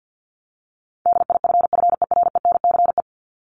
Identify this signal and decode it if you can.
Morse code, heard as a keyed tone, about 50 wpm, 700 Hz. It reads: DIFFERENCE
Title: Top 751-1000 Words 50wpm